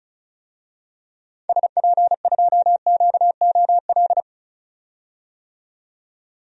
Morse code, heard as a keyed tone, about 34 wpm, 700 Hz. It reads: SP2QOL
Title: DXCC-100 Call Signs - 6 Characters - 35wpm